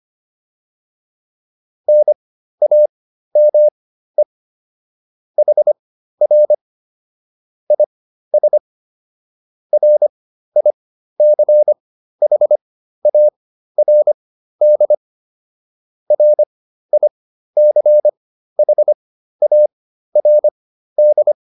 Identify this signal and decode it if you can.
Morse code, heard as a keyed tone, about 25 wpm, spaced as for 13 wpm, 600 Hz. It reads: NAME HR IS RICHARD RICHARD